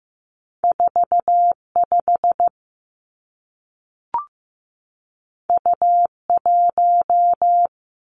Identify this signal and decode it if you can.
Morse code, heard as a keyed tone, about 15 wpm, 700 Hz. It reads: 45 U1
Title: Two Challenging Letter-Number Combinations 15wpm